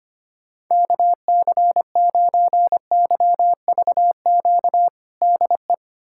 Morse code, heard as a keyed tone, about 25 wpm, 700 Hz. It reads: KC9YVQ DE
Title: USA Callsign DE Callsign - 6 Characters - 25wpm